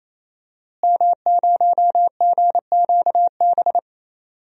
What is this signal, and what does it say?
Morse code, heard as a keyed tone, about 27 wpm, 700 Hz. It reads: M0GQB
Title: DXCC-100 Call Signs - 5 Characters - Repeated 3x - 28wpm